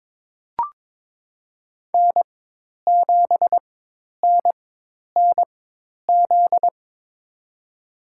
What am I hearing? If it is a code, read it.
Morse code, heard as a keyed tone, about 22 wpm, spaced as for 10 wpm, 700 Hz. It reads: N7NNZ
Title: USA Call Signs - 4x Character Spacing - 22wpm